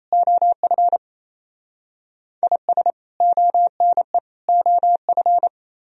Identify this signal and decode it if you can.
Morse code, heard as a keyed tone, about 34 wpm, 700 Hz. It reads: OF IS ONE OF
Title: Speed-Racing - Sets of 3 Words 28wpm